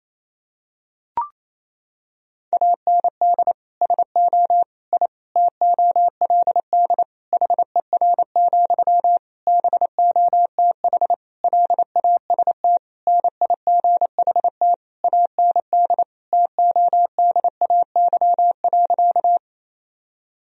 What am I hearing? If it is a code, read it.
Morse code, heard as a keyed tone, about 28 wpm, 700 Hz. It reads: AND SO I TOLD HER, BOTH LAST NIGHT AND TODAY.
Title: Sentences from Top 1000 Words - Review Part 4 - 28wpm